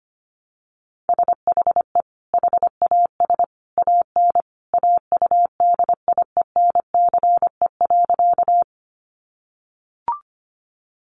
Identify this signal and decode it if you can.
Morse code, heard as a keyed tone, about 25 wpm, 700 Hz. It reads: SHE HAS AN AUDIENCE.
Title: Sentences from Top 800 Words 25wpm